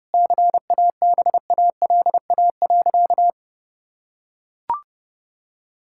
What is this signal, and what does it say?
Morse code, heard as a keyed tone, about 30 wpm, 700 Hz. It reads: CABALA.
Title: Pangram Challenge v2 - 30wpm